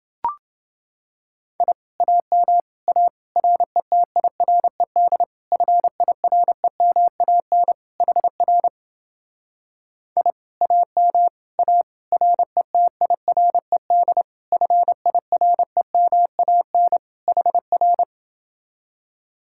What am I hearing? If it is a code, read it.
Morse code, heard as a keyed tone, about 29 wpm, 700 Hz. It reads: I AM A RETIRED FIREMAN HR I AM A RETIRED FIREMAN HR